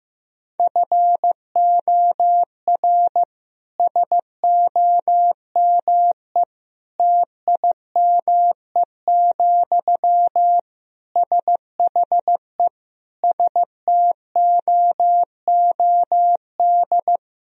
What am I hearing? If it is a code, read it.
Morse code, heard as a keyed tone, about 15 wpm, 700 Hz. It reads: FOR SOME TIME, SHE STOOD
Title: Sentences from Top 1000 Words - Review Part 2 - 15wpm